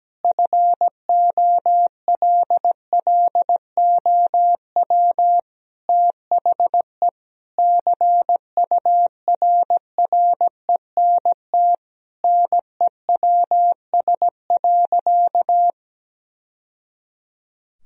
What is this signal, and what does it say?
Morse code, heard as a keyed tone, about 17 wpm, 700 Hz. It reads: FOLLOW THE CURRENT NEWS.
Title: Sentences from Top 1000 Words - Review Part 4 - 17wpm